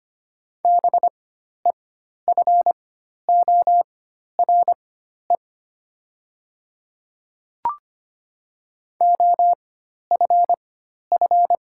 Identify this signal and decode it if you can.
Morse code, heard as a keyed tone, about 25 wpm, spaced as for 12 wpm, 690 Hz. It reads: BEFORE OFF
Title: 4x Character Spacing - Top 200 Words - 25wpm